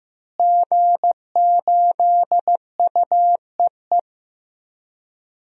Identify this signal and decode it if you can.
Morse code, heard as a keyed tone, about 15 wpm, 700 Hz. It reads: G8UEE
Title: DXCC-100 Call Signs - 5 Characters - Repeated 3x - 15wpm